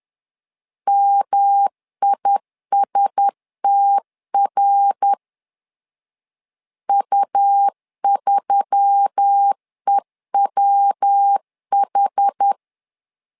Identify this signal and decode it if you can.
Morse code, heard as a keyed tone, about 10 wpm, 800 Hz. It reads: MISTR U3EWH